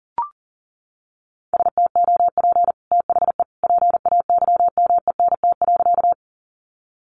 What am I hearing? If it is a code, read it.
Morse code, heard as a keyed tone, about 40 wpm, 700 Hz. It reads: STOP THE PAYMENT.
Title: Sentences from Top 1200 Words 40wpm